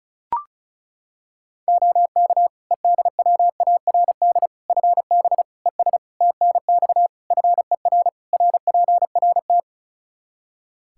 Morse code, heard as a keyed tone, about 36 wpm, 700 Hz. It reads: OK EDWARD FB ES TNX FER RPRT
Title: Practice QSO - Exchange 2 - Info - 35wpm